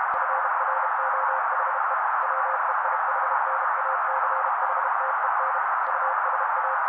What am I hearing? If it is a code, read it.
Morse code, heard as a keyed tone, about 32 wpm, 530 Hz. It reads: FROHE WEIHNACHTEN USA
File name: daily / 31wpm noise